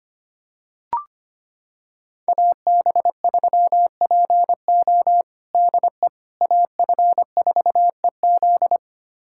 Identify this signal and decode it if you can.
Morse code, heard as a keyed tone, about 25 wpm, 700 Hz. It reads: AB3PO DE AF4EZ
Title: USA Callsign DE Callsign - 5 Characters - 25wpm